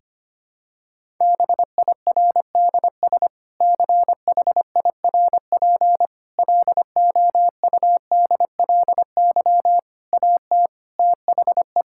Morse code, heard as a keyed tone, about 25 wpm, 700 Hz. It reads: BIRDS CHIRP LOUDLY AT THE
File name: practice / 25wpm / Difficult Sentences with Hints - 25wpm